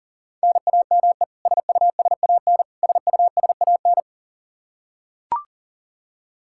Morse code, heard as a keyed tone, about 40 wpm, 700 Hz. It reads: NAME SUSAN SUSAN
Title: Practice QSO - Exchange 1 - Beginning - 40wpm